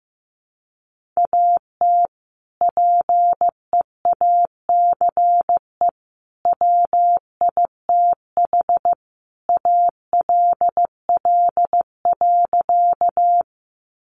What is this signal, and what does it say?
Morse code, heard as a keyed tone, about 15 wpm, 700 Hz. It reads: AT PEACE WITH ALL.
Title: Sentences from Top 1500 Words 15wpm